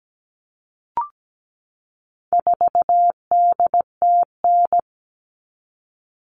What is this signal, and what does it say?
Morse code, heard as a keyed tone, about 17 wpm, 700 Hz. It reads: 4DTN
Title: All Cut Numbers - 4-Digit 17wpm